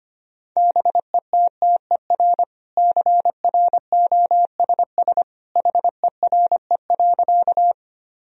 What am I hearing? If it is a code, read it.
Morse code, heard as a keyed tone, about 25 wpm, 700 Hz. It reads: BETTER CROSS HERE.
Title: Sentences from Top 1000 Words - Review Part 1 - 25wpm